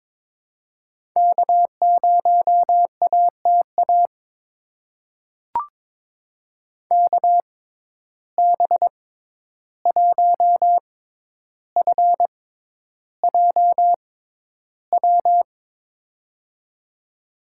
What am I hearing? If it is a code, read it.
Morse code, heard as a keyed tone, about 22 wpm, spaced as for 11 wpm, 700 Hz. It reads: K0ATA KB1FJW
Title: USA Call Signs - 6x Character Spacing - 22wpm